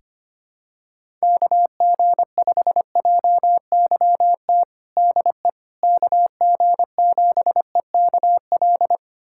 Morse code, heard as a keyed tone, about 25 wpm, 700 Hz. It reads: KG5JYT DE KG7EKL
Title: USA Callsign DE Callsign 25wpm